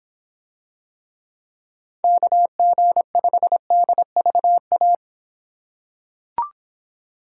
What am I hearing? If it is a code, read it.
Morse code, heard as a keyed tone, about 25 wpm, 700 Hz. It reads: KG5DVA